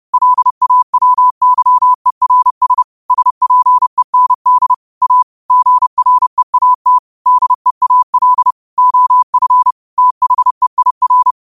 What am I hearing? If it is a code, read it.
Morse code, heard as a keyed tone, about 30 wpm, 1000 Hz. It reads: LAWYERS SPEND A GREAT DEAL OF THEIR